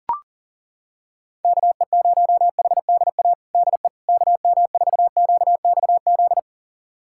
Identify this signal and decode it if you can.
Morse code, heard as a keyed tone, about 40 wpm, 700 Hz. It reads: KE0HDA DE KM4QXZ
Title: USA Callsign DE Callsign - 6 Characters - 40wpm